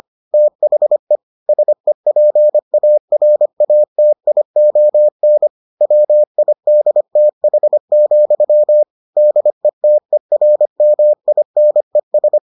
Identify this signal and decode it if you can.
Morse code, heard as a keyed tone, about 25 wpm, 600 Hz. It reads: THE SEPARATION WIDTH, DETERMINES